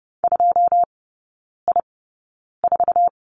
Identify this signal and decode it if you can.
Morse code, heard as a keyed tone, about 30 wpm, spaced as for 9 wpm, 700 Hz. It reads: 2I4
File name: Lesson 17e - Letter u - Three Characters 30wpm